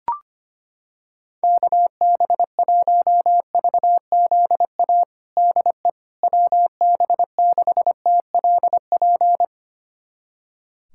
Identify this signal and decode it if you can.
Morse code, heard as a keyed tone, about 25 wpm, 700 Hz. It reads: KB1VZA DE WB6TLP
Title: USA Callsign DE Callsign - 6 Characters - 25wpm